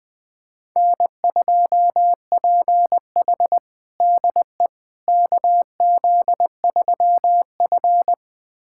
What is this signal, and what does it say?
Morse code, heard as a keyed tone, about 20 wpm, 700 Hz. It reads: N2PH DE KZ3F
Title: USA Callsign DE Callsign - 4 Characters - 20wpm